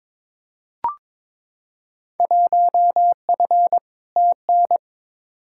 Morse code, heard as a keyed tone, about 22 wpm, 700 Hz. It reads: 1F TN
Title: Field Day Exchanges - 22wpm